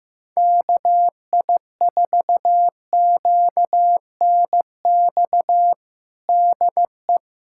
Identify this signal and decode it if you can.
Morse code, heard as a keyed tone, about 16 wpm, 700 Hz. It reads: KI4QNX DE
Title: USA Callsign DE Callsign - 6 Characters - 15wpm